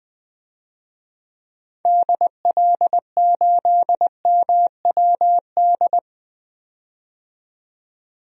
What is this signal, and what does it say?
Morse code, heard as a keyed tone, about 20 wpm, 700 Hz. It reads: DL8MWD
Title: DXCC-100 Call Signs - 6 Characters - 20wpm